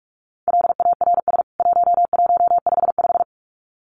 Morse code, heard as a keyed tone, about 44 wpm, 700 Hz. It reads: LARS 1155
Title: CWT Exchange 45wpm